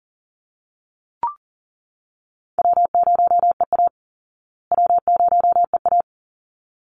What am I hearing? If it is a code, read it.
Morse code, heard as a keyed tone, about 40 wpm, 700 Hz. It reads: W0EA W0EA
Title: Call Signs Repeated 3x - 40wpm